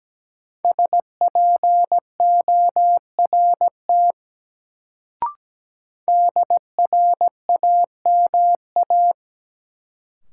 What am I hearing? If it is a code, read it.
Morse code, heard as a keyed tone, about 17 wpm, 700 Hz. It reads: SPORT DRAMA
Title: Sentences from Top 1500 Words - New Word Drill Alt - 17wpm